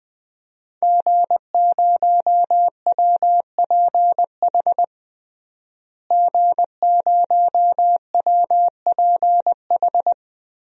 Morse code, heard as a keyed tone, about 21 wpm, 700 Hz. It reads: G0WPH G0WPH
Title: DXCC-100 Call Signs - 5 Characters - Repeated 3x - 20wpm